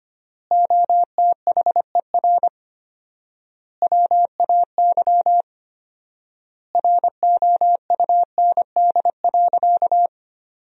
Morse code, heard as a keyed tone, about 25 wpm, 700 Hz. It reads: OTHER WAY ROUND.
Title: 4x Word-Spacing - Sentences from Top 200 Words - 25wpm